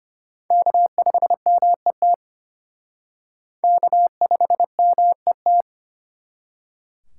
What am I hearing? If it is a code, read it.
Morse code, heard as a keyed tone, about 34 wpm, 700 Hz. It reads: K5MET K5MET